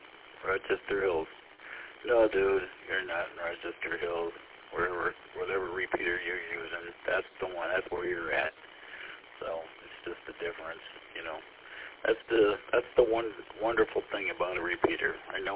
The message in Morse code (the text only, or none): none